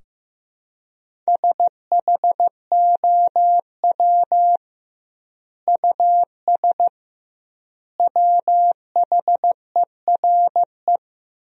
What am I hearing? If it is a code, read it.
Morse code, heard as a keyed tone, about 15 wpm, 700 Hz. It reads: SHOW US WHERE
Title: 2x Word-Spacing - Sentences from Top 200 Words - 15wpm